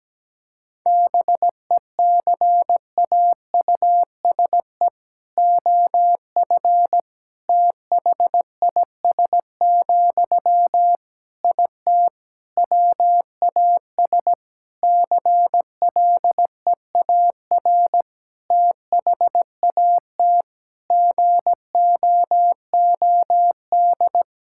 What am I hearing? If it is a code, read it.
Morse code, heard as a keyed tone, about 17 wpm, 700 Hz. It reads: BECAUSE OF THIS, IT WAS CLEAR THAT GOOD
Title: Sentences from Top 1000 Words - Review Part 6 - 17wpm